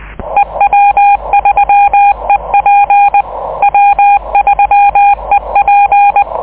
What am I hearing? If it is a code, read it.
Morse code, heard as a keyed tone, about 20 wpm, 810 Hz. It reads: E W3EP W3EP
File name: 6M audio / w3ep cw cq cycle23